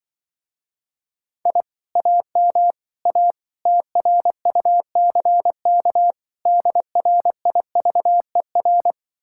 Morse code, heard as a keyed tone, about 24 wpm, 700 Hz. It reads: I AM A TRUCK DRIVER
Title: Speed-Racing Common QSO Phrases - 20wpm